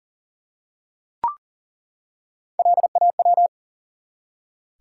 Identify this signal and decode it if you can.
Morse code, heard as a keyed tone, about 40 wpm, 700 Hz. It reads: LAW
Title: Three Letter Words - 40wpm